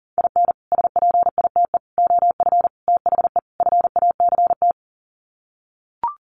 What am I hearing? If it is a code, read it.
Morse code, heard as a keyed tone, about 40 wpm, 700 Hz. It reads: IN SPITE OF THE FACT
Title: Sets of 5 Words Encore2 40wpm